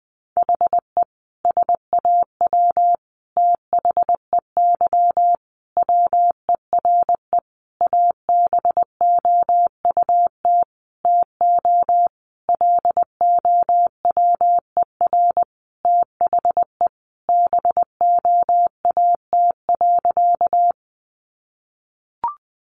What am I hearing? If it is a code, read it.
Morse code, heard as a keyed tone, about 20 wpm, 700 Hz. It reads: HE SAW THEY WERE ABOUT TO LOWER THE BOAT.
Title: Sentences from Top 700 Words 20wpm